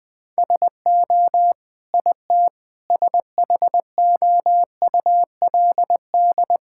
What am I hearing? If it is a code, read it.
Morse code, heard as a keyed tone, about 20 wpm, 700 Hz. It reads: SO IT SHOULD
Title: Sentences with 8-Word Phrases - 20wpm